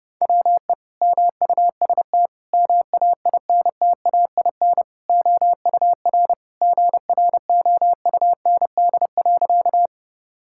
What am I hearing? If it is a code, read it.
Morse code, heard as a keyed tone, about 30 wpm, 700 Hz. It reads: WE MUST MAINTAIN OUR GROUND.
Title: Sentences from Top 1000 Words - Review Part 5 - 30wpm